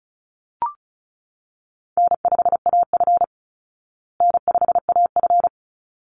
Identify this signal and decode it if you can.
Morse code, heard as a keyed tone, about 34 wpm, 700 Hz. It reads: N5AF N5AF
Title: Easy Call Signs Repeated 2x - 35wpm